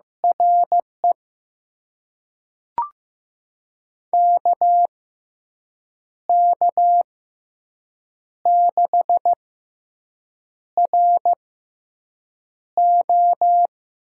Morse code, heard as a keyed tone, about 15 wpm, spaced as for 6 wpm, 700 Hz. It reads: RE KK6RO
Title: USA Call Signs - 6x Character Spacing - 15wpm